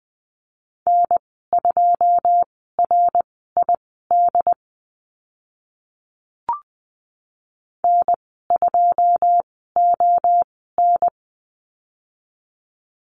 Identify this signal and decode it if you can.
Morse code, heard as a keyed tone, about 20 wpm, spaced as for 14 wpm, 700 Hz. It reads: N2RID N2ON
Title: 088 Lesson 15g - Number 5 - Call Signs 20wpm